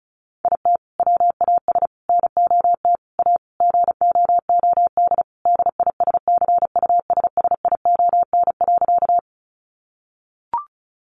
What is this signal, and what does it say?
Morse code, heard as a keyed tone, about 34 wpm, 700 Hz. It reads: IT WAS NOT A GOOD DISCUSSION.
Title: Sentences from Top 1100 Words 35wpm